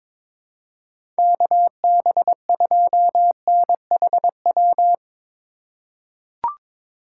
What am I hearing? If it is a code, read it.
Morse code, heard as a keyed tone, about 23 wpm, 700 Hz. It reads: KB2NHW
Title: USA Call Signs - 6 Characters - 2x Character Spacing - 22wpm